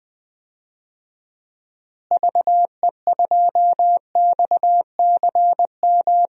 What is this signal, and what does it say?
Morse code, heard as a keyed tone, about 21 wpm, 700 Hz. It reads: VE2XCM
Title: DXCC-100 Call Signs - 6 Characters - Repeated 2x - 20wpm